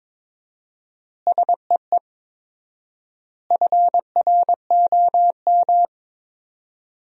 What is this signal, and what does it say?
Morse code, heard as a keyed tone, about 22 wpm, 700 Hz. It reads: SEE FROM
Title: 4x Word-Spacing - Sentences from Top 100 Words - 22wpm